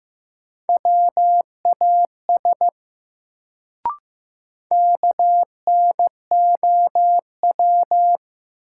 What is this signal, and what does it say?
Morse code, heard as a keyed tone, about 15 wpm, 700 Hz. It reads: WAS KNOW
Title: Sentences from Top 100 Words - New Word Drill - 15wpm